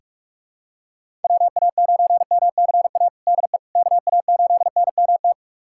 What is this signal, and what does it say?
Morse code, heard as a keyed tone, about 43 wpm, 700 Hz. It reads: WA9MCA DE KA8NMT
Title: USA Callsign DE Callsign - 6 Characters - 45wpm